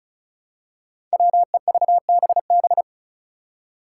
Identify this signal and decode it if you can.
Morse code, heard as a keyed tone, about 36 wpm, 700 Hz. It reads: WEVBB
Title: Five Challenging Letter-Number Combinations 35wpm